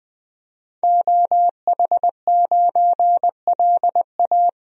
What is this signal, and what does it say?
Morse code, heard as a keyed tone, about 20 wpm, 700 Hz. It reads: OH9LA
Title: DXCC-100 Call Signs - 5 Characters - 20wpm